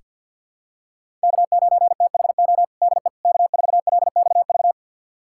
Morse code, heard as a keyed tone, about 48 wpm, 700 Hz. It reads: K9THO DE K4LXV
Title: USA Callsign DE Callsign 50wpm